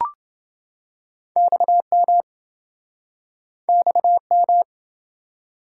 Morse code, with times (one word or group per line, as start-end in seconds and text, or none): 1.4-2.2 XM
3.7-4.6 XM